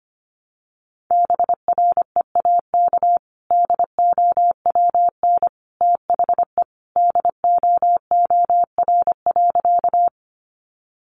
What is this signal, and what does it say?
Morse code, heard as a keyed tone, about 25 wpm, 700 Hz. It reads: BREAK DOWN THE DOOR.
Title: Sentences from Top 900 Words 25wpm